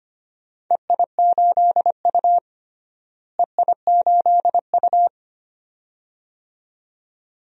Morse code, heard as a keyed tone, about 25 wpm, 700 Hz. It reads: EI8U EI8U
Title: DXCC-100 Call Signs - 4 Characters - Repeated 2x - 25wpm